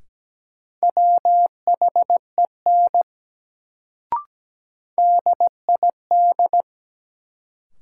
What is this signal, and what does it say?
Morse code, heard as a keyed tone, about 17 wpm, 700 Hz. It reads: WHEN DID
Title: Sentences from Top 100 Words - New Word Drill - 17wpm